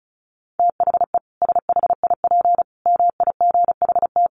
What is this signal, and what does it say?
Morse code, heard as a keyed tone, about 34 wpm, 700 Hz. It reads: THE SHIP MIGHT
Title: Sentences from Top 1400 Words 35wpm